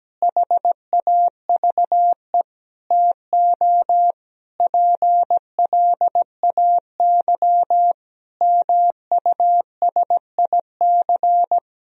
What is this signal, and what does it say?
Morse code, heard as a keyed tone, about 17 wpm, 700 Hz. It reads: HAVE TO PLAY MUSIC
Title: Sentences from Top 1000 Words - Review Part 1 - 17wpm